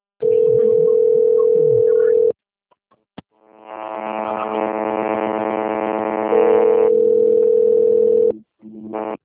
Ringback tone (North America), 440 + 480 Hz.
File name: rg-1600-unknown- - - .19310